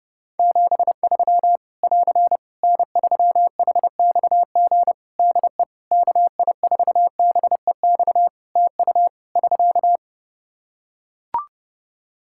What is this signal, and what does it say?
Morse code, heard as a keyed tone, about 30 wpm, 700 Hz. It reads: 73 <AR> N3HXG DE KI4BEX TU <SK>
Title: K1USN SST QSO Practice 30wpm